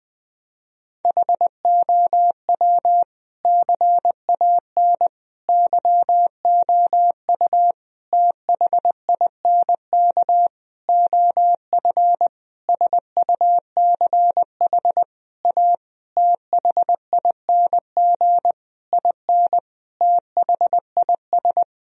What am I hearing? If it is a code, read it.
Morse code, heard as a keyed tone, about 21 wpm, 700 Hz. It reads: HOW CAN YOU THINK OF SUCH A THING IN THIS
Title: Sentences from Top 1000 Words - Review Part 3 - 20wpm